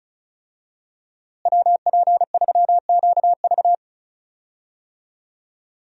Morse code, heard as a keyed tone, about 36 wpm, 700 Hz. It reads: WP3QV
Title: USA Call Signs - 5 Characters - Repeated 3x - 35wpm